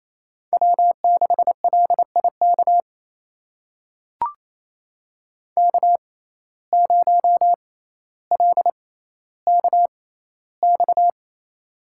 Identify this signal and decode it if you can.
Morse code, heard as a keyed tone, about 28 wpm, spaced as for 14 wpm, 700 Hz. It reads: W6LIK K0LKX
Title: USA Call Signs - 5 Characters - 6x Character Spacing - 28wpm